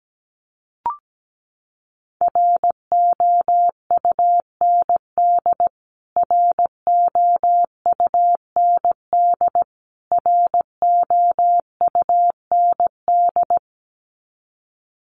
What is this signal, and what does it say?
Morse code, heard as a keyed tone, about 17 wpm, 700 Hz. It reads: ROUND ROUND ROUND
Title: Top 200 Words Repeated 17wpm